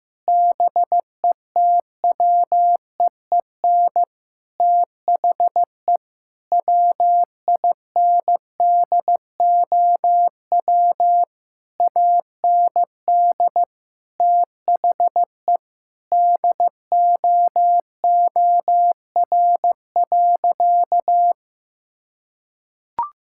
Morse code, heard as a keyed tone, about 16 wpm, 700 Hz. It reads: BETWEEN THE WINDOW AND THE DOOR.
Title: Sentences from Top 1600 Words 15wpm